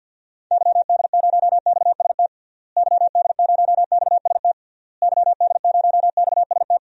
Morse code, heard as a keyed tone, about 48 wpm, 700 Hz. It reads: YD0XST YD0XST YD0XST
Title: DXCC-100 Call Signs - 6 Characters - Repeated 3x - 50wpm